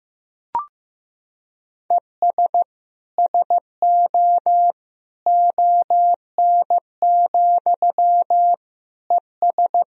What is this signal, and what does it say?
Morse code, heard as a keyed tone, about 15 wpm, 700 Hz. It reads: ES SO ON, ES
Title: Sentences from Top 100 Words and Abbreviations 15wpm